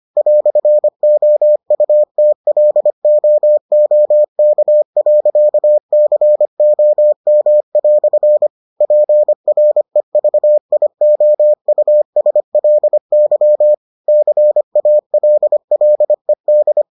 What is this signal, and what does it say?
Morse code, heard as a keyed tone, about 25 wpm, 600 Hz. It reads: "OUTLOOK.COM" PREVIOUSLY CALLED